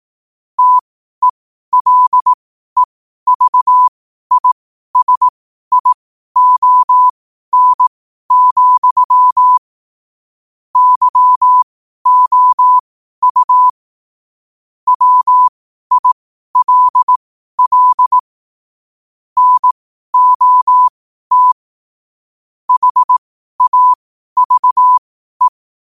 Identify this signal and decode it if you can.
Morse code, heard as a keyed tone, about 18 wpm, spaced as for 12 wpm, 1000 Hz. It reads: TELEVISION, YOU WILL NOT HAVE
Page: qotd-at-12-wpm